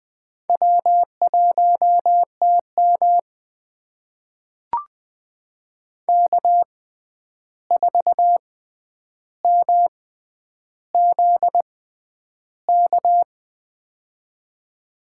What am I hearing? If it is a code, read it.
Morse code, heard as a keyed tone, about 20 wpm, spaced as for 10 wpm, 700 Hz. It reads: W1TM K4MZK
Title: USA Call Signs - 6x Character Spacing - 20wpm